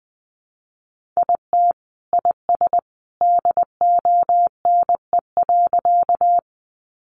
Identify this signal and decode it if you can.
Morse code, heard as a keyed tone, about 20 wpm, 700 Hz. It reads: IT IS DONE.
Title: Sentences from Top 500 Words 20wpm